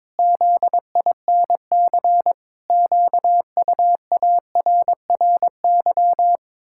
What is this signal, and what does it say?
Morse code, heard as a keyed tone, about 22 wpm, 700 Hz. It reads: ZINC QUARRY
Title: Pangram Challenge v2 - 22wpm